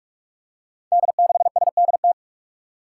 Morse code, heard as a keyed tone, about 42 wpm, 700 Hz. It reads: D6SDT
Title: Five Challenging Letter-Number Combinations 45wpm